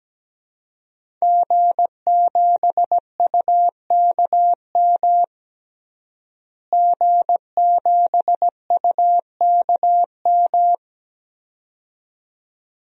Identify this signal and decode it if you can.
Morse code, heard as a keyed tone, about 17 wpm, 700 Hz. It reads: G7UKM G7UKM
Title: DXCC-100 Call Signs - 5 Characters - Repeated 2x - 17wpm